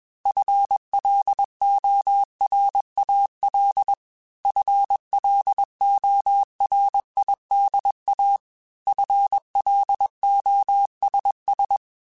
Morse code, heard as a keyed tone, about 21 wpm, 800 Hz. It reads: FLORAL FLORIDA FLOSS
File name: FL Runs / FL Runs mp3s / FL Runs @20